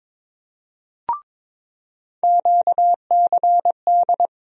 Morse code, heard as a keyed tone, about 22 wpm, 700 Hz. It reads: QCD
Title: USA Call Sign Postfixes - 3 Letters 22wpm